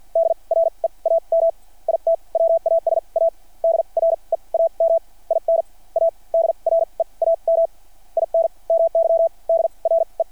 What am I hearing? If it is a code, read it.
Morse code, heard as a keyed tone, about 53 wpm, spaced as for 37 wpm, 660 Hz. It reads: DREAM IT WAS A DREAM IN A DREAM IN MY DRE
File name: archived / CRSnet